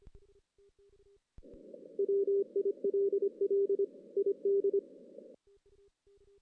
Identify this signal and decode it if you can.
Morse code, heard as a keyed tone, about 25 wpm, 400 Hz. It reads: WILL ID
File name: 2016 JAN CW NAQP / 2016 Jan naqp k5ot - 0-5